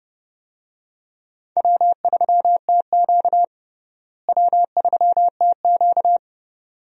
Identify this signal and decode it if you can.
Morse code, heard as a keyed tone, about 30 wpm, 700 Hz. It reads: W3TQ W3TQ
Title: USA Call Signs - 4 Characters - Repeated 2x - 30wpm